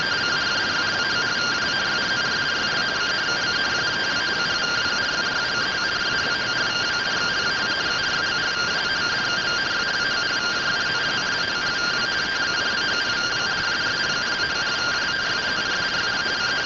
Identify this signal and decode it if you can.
Two-tone teleprinter signal, about 170 Hz shift.